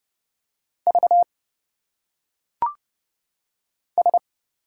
Morse code, heard as a keyed tone, about 30 wpm, 700 Hz. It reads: V S
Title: Single Letters - Phonetic Pronunciation 30wpm